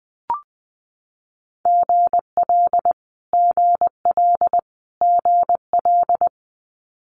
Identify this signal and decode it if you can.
Morse code, heard as a keyed tone, about 20 wpm, 700 Hz. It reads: GL GL GL
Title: Multi-letter Phonemes Repeated 3x 20wpm